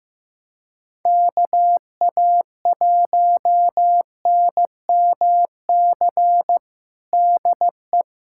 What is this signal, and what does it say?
Morse code, heard as a keyed tone, about 15 wpm, 700 Hz. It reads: KA1NMC DE
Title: USA Callsign DE Callsign 15wpm